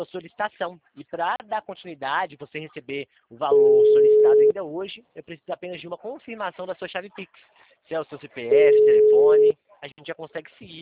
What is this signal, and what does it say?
Ringback tone (Europe, 425 Hz), with speech.